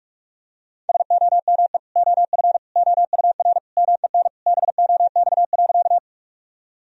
Morse code, heard as a keyed tone, about 41 wpm, 700 Hz. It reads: SOME OF OUR MEN BOX.
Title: Sentences from Top 1000 Words - Review Part 2 - 45wpm